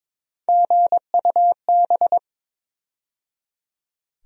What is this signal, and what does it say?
Morse code, heard as a keyed tone, about 22 wpm, 700 Hz. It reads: GUB
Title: USA Call Sign Postfixes - 3 Letters 22wpm